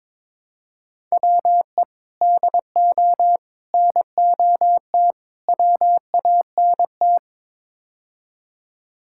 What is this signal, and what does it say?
Morse code, heard as a keyed tone, about 22 wpm, 700 Hz. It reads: WE DO NOT WANT
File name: practice / 22wpm / Build A Phrase - 5-Word Phrases - 22wpm